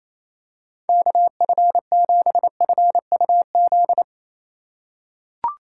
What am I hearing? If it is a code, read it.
Morse code, heard as a keyed tone, about 28 wpm, 700 Hz. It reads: KF7FUZ
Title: USA Call Signs - 6 Characters - 3x Character Spacing - 28wpm